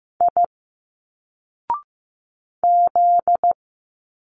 Morse code, heard as a keyed tone, about 15 wpm, 700 Hz. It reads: I Z
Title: Single Letters 15wpm